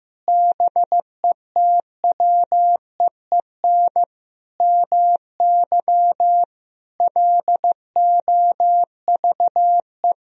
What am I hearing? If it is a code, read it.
Morse code, heard as a keyed tone, about 16 wpm, 700 Hz. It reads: BETWEEN MY LOVE
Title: Sentences from Top 1000 Words - Review Part 1 - 15wpm